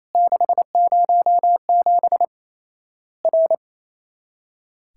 Morse code, heard as a keyed tone, about 28 wpm, 700 Hz. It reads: 607 R
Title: Worked All Europe DX Contest - QTC Exchange - 28wpm